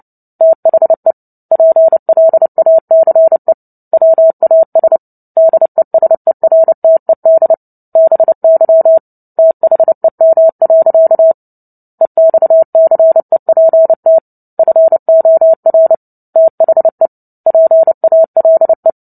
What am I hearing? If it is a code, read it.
Morse code, heard as a keyed tone, about 30 wpm, 650 Hz. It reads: THE PLACE WAS DESERTED BY THEM. EXCEPT FOR THE PALE
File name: Book Two / 03-Chapter 3 - 29 WPM